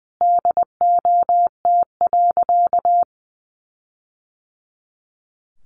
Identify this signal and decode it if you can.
Morse code, heard as a keyed tone, about 20 wpm, 700 Hz. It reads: DOT.
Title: Sentences from Top 1500 Words 20wpm